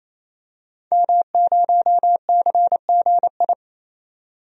Morse code, heard as a keyed tone, about 28 wpm, 700 Hz. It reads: M0CGI
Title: DXCC-100 Call Signs - 5 Characters - 28wpm